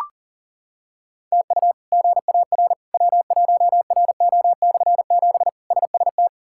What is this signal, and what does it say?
Morse code, heard as a keyed tone, about 37 wpm, 700 Hz. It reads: TU GAR W1RO/7 SST
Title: K1USN SST QSO Practice - 40wpm